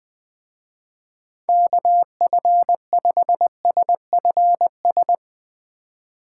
Morse code, heard as a keyed tone, about 20 wpm, 700 Hz. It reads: KF5SFS
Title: Speed Racing - USA 2x3 Call Signs 17wpm